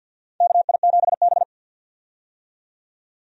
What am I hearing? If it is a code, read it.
Morse code, heard as a keyed tone, about 49 wpm, 700 Hz. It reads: KI7B
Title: USA Call Signs - 4 Characters - 50wpm